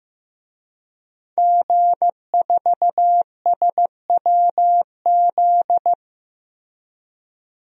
Morse code, heard as a keyed tone, about 15 wpm, 700 Hz. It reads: G4SWZ
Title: DXCC-100 Call Signs - 5 Characters - Repeated 3x - 15wpm